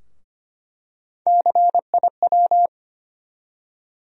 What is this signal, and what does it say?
Morse code, heard as a keyed tone, about 25 wpm, 700 Hz. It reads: CIW
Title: USA Call Sign Postfixes - 3 Letters 25wpm